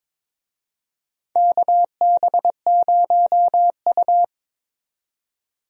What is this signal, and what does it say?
Morse code, heard as a keyed tone, about 22 wpm, 700 Hz. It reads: KB0U
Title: USA Call Signs - 4 Characters - 22wpm